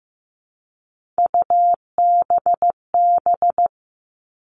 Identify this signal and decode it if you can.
Morse code, heard as a keyed tone, about 15 wpm, 700 Hz. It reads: UBB
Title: All Cut Numbers - 3-Digit 15wpm